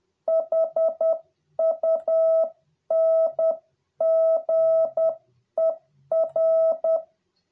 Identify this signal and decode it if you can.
Morse code, heard as a keyed tone, about 10 wpm, 650 Hz. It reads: HUNGER